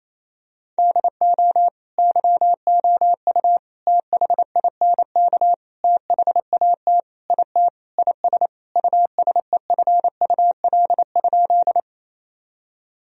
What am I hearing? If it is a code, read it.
Morse code, heard as a keyed tone, about 28 wpm, 700 Hz. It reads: DO YOU THINK THAT IT IS USEFUL?
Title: Sentences from Top 1000 Words - Review Part 4 - 28wpm